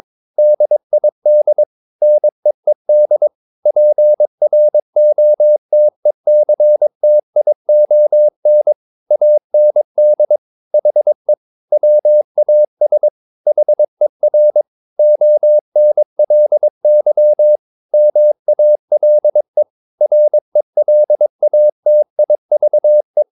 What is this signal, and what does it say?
Morse code, heard as a keyed tone, about 22 wpm, 600 Hz. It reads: DID NEED PROTECTION AND HE WAS HER ONLY MALE RELATIVE
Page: Romanen Never af Ken Follet på engelsk.